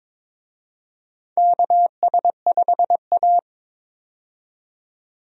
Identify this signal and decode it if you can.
Morse code, heard as a keyed tone, about 22 wpm, 700 Hz. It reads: KS5A
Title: USA Call Signs - 4 Characters - 22wpm